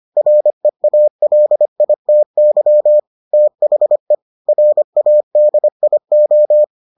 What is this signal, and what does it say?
Morse code, heard as a keyed tone, about 25 wpm, 600 Hz. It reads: REALITY THE RADIO